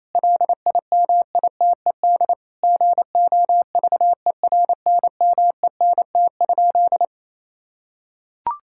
Morse code, heard as a keyed tone, about 28 wpm, 700 Hz. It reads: LIMITED GOVERNMENT?
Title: Sentences from Top 1000 Words - Review Part 6 - 28wpm